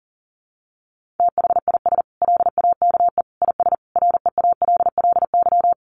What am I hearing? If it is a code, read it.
Morse code, heard as a keyed tone, about 40 wpm, 700 Hz. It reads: THIS LAKE IS REALLY
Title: Sentences from Top 1100 Words 40wpm